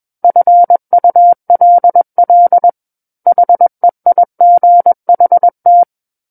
Morse code, heard as a keyed tone, about 21 wpm, 700 Hz. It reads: FULL HEIGHT
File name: Book One / 12-Chapter 12 - 21 WPM